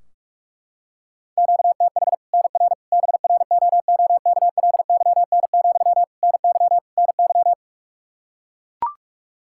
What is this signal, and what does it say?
Morse code, heard as a keyed tone, about 42 wpm, 700 Hz. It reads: QTH NR BROOKLYN, NY NY
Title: Practice QSO - Exchange 1 - Beginning - 45wpm